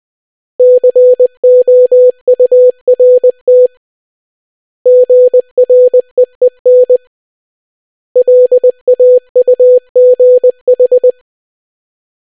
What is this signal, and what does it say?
Morse code, heard as a keyed tone, about 20 wpm, 500 Hz. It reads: COURT GREEN LAUGH